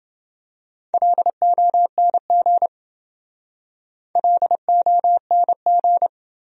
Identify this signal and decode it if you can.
Morse code, heard as a keyed tone, about 28 wpm, 700 Hz. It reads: LONG LONG